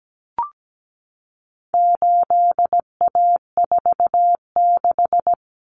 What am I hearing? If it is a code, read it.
Morse code, heard as a keyed tone, about 17 wpm, 700 Hz. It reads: 8A46
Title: Common Cut Numbers - 4-Digit 17wpm